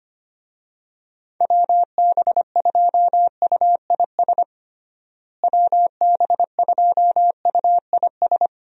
Morse code, heard as a keyed tone, about 25 wpm, 700 Hz. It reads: WB2UIS WB2UIS
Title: USA Call Signs - 6 Characters - Repeated 2x - 25wpm